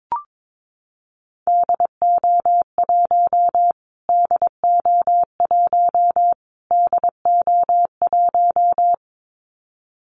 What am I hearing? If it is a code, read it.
Morse code, heard as a keyed tone, about 22 wpm, 700 Hz. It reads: DO1 DO1 DO1
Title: Callsign Prefixes for Top 25 DXCC - Repeated 3x - 22wpm